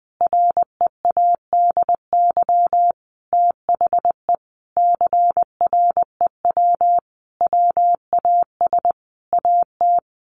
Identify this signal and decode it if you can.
Morse code, heard as a keyed tone, about 20 wpm, 700 Hz. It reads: READY THE CREW WAS AT
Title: Sentences from Top 1500 Words 20wpm